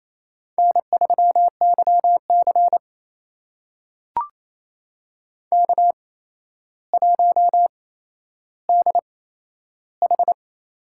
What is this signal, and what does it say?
Morse code, heard as a keyed tone, about 28 wpm, spaced as for 12 wpm, 700 Hz. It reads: N3YC K1DH
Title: USA Call Signs - 4 Characters - 8x Character Spacing - 28wpm